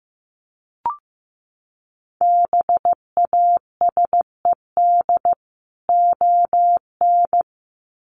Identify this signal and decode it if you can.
Morse code, heard as a keyed tone, about 15 wpm, 700 Hz. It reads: BASED ON
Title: Sets of 2 Words 15wpm